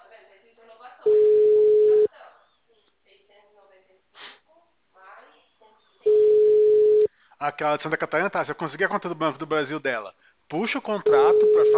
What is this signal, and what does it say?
Ringback tone (Europe, 425 Hz), with speech.
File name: out- -2001- - - .1776